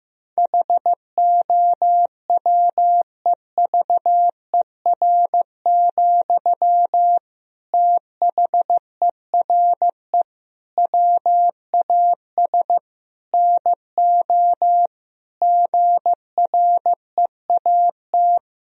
Morse code, heard as a keyed tone, about 15 wpm, 700 Hz. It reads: HOWEVER, THERE WAS NO GREAT
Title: Sentences from Top 1000 Words - Review Part 4 - 15wpm